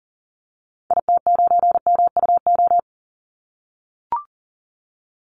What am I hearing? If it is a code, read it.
Morse code, heard as a keyed tone, about 40 wpm, 700 Hz. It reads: IT9MUO
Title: Call Signs - 40wpm